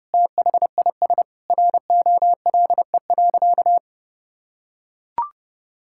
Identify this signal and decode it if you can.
Morse code, heard as a keyed tone, about 32 wpm, 700 Hz. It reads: THIS ROLE.
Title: Sentences from Top 1000 Words - Review Part 6 - 30wpm